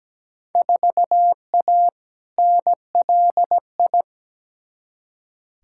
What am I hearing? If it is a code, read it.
Morse code, heard as a keyed tone, about 17 wpm, 700 Hz. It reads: 4A NLI
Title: Field Day Exchanges Encore - 17wpm